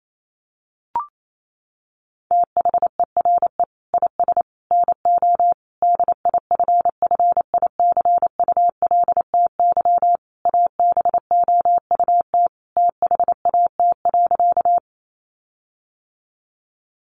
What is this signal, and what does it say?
Morse code, heard as a keyed tone, about 28 wpm, 700 Hz. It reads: THERE IS NO DIFFICULTY ABOUT THAT.
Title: Sentences from Top 1500 Words 28wpm